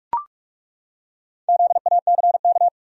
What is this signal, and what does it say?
Morse code, heard as a keyed tone, about 43 wpm, 700 Hz. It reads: ZACK
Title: USA Common Names - Max 4 Letters - 45wpm